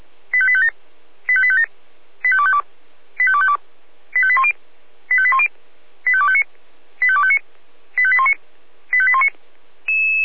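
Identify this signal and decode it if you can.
Five-tone selective call (ZVEI), heard as quick runs of five tones, tones of about 70 ms. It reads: 85857 85857 85252 85252 85810 85810 85259 85259 85819 85819
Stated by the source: ZVEI five-tone selective call received by SDR.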